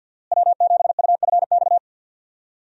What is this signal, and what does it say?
Morse code, heard as a keyed tone, about 48 wpm, 700 Hz. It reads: W7UFX
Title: USA Call Signs - 5 Characters - Repeated 2x - 50wpm